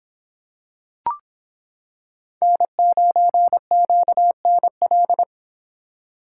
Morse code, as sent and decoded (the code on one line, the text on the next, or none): -. ----. --.- -. .-..
N9QNL